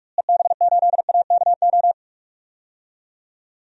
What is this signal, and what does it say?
Morse code, heard as a keyed tone, about 46 wpm, 700 Hz. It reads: EB8AKO
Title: DXCC-100 Call Signs - 6 Characters - Repeated 3x - 45wpm